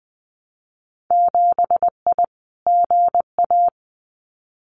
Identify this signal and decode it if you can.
Morse code, heard as a keyed tone, about 19 wpm, 700 Hz. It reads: 7I GA
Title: Winter Field Day Exchanges Encore 20wpm